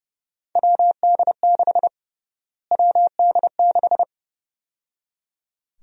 Morse code, heard as a keyed tone, about 31 wpm, 700 Hz. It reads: WD6 WD6
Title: USA Call Sign Prefixes - Repeated 2x - 30wpm